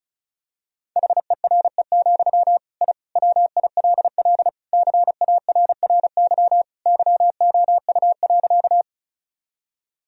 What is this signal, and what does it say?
Morse code, heard as a keyed tone, about 34 wpm, 700 Hz. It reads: HERE, I WILL CARRY YOU.
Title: Sentences from Top 1000 Words - Review Part 1 - 35wpm